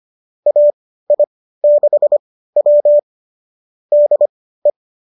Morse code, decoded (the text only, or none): AI6W DE